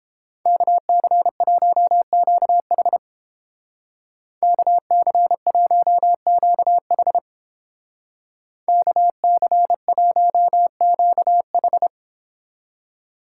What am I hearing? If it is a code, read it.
Morse code, heard as a keyed tone, about 29 wpm, 700 Hz. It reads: KC1QH KC1QH KC1QH